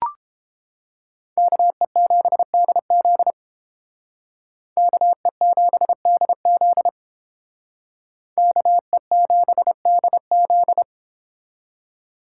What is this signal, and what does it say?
Morse code, heard as a keyed tone, about 29 wpm, 700 Hz. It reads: KE7DZ KE7DZ KE7DZ